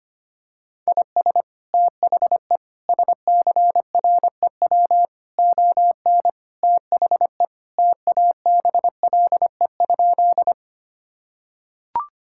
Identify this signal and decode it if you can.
Morse code, heard as a keyed tone, about 25 wpm, 700 Hz. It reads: IS THE SCREW ON THE TABLE?
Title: Sentences from Top 1600 Words 25wpm